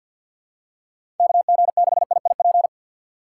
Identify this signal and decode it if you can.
Morse code, heard as a keyed tone, about 53 wpm, 700 Hz. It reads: KG6IIP
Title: K1USN SST QSO Practice - 50wpm